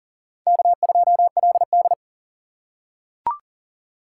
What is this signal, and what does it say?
Morse code, heard as a keyed tone, about 40 wpm, 690 Hz. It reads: K2LD
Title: USA Call Signs - 4 Characters - 2x Character Spacing - 40wpm